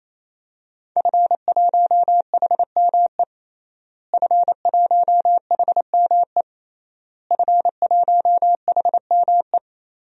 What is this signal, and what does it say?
Morse code, heard as a keyed tone, about 28 wpm, 700 Hz. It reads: F1HME F1HME F1HME
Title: DXCC-100 Call Signs - 5 Characters - Repeated 3x - 28wpm